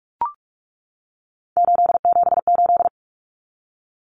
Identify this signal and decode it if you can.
Morse code, heard as a keyed tone, about 44 wpm, 700 Hz. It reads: 878
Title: Common Cut Numbers - 3-Digit 45wpm